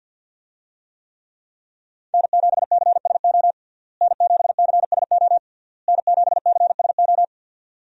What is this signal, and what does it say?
Morse code, heard as a keyed tone, about 53 wpm, 700 Hz. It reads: N7CSO N7CSO N7CSO
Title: USA Call Signs - 5 Characters - Repeated 3x - 50wpm